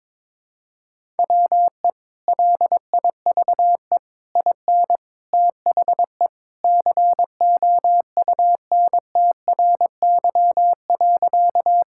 Morse code, heard as a keyed tone, about 21 wpm, 700 Hz. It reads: WE LIVE IN THE COUNTRY.
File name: practice / 22wpm / Sentences with 3-Word Phrases - 22wpm